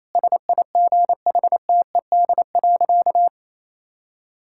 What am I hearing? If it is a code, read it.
Morse code, heard as a keyed tone, about 28 wpm, 700 Hz. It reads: SIGHTED.
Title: Pangram Challenge v2 - 28wpm